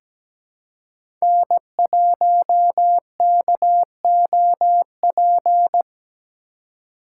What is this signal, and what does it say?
Morse code, heard as a keyed tone, about 17 wpm, 700 Hz. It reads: N1KOP
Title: USA Call Signs - 5 Characters - 2x Character Spacing - 17wpm